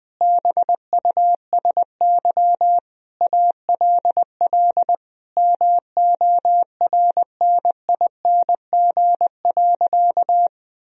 Morse code, heard as a keyed tone, about 21 wpm, 700 Hz. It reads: BUSY ALL MORNING.
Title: Sentences from Top 1000 Words - Review Part 3 - 20wpm